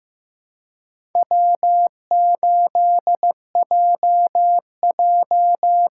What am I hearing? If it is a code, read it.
Morse code, heard as a keyed tone, about 15 wpm, 700 Hz. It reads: W8JJ
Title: USA Call Signs - 5 Characters - 6x Character Spacing - 15wpm